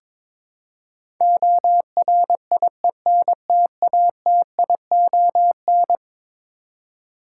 Morse code, heard as a keyed tone, about 22 wpm, 700 Hz. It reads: ORIENTATION
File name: practice / 22wpm / Eleven Letter Words - 22wpm